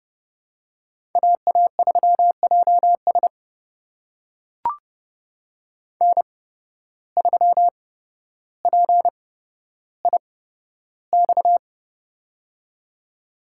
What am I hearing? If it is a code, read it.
Morse code, heard as a keyed tone, about 30 wpm, spaced as for 13 wpm, 700 Hz. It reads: AA3JS N3PIX
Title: USA Call Signs - 5 Characters - 8x Character Spacing - 30wpm